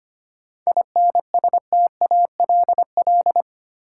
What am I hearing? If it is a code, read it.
Morse code, heard as a keyed tone, about 25 wpm, 700 Hz. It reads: INSTALL
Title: Sentences from Top 1600 Words - New Word Drill Alt - 25wpm